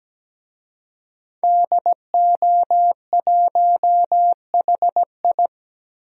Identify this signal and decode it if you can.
Morse code, heard as a keyed tone, about 17 wpm, 700 Hz. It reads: DO1HI
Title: DXCC-100 Call Signs - 5 Characters - Repeated 2x - 17wpm